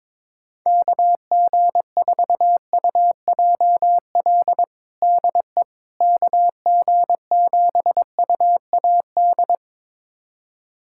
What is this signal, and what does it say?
Morse code, heard as a keyed tone, about 22 wpm, 700 Hz. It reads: KG4UJL DE KG7UAD
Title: USA Callsign DE Callsign 22wpm